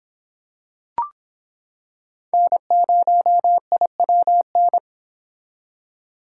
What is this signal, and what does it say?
Morse code, heard as a keyed tone, about 26 wpm, 700 Hz. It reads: N0IWN